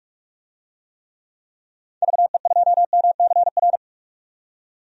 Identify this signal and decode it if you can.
Morse code, heard as a keyed tone, about 43 wpm, 700 Hz. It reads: VE2MCR
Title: DXCC-100 Call Signs - 6 Characters - 45wpm